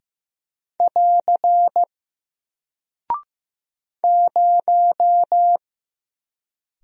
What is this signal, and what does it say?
Morse code, heard as a keyed tone, about 15 wpm, 700 Hz. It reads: <AR> 0
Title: Single Character (Letter:Number:Punctuation:Prosign) 15wpm